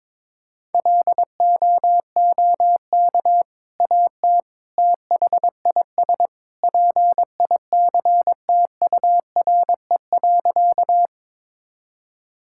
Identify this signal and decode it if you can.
Morse code, heard as a keyed tone, about 22 wpm, 700 Hz. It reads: LOOK AT THIS PICTURE.
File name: practice / 22wpm / Sentences with 2-Word Phrases - 22wpm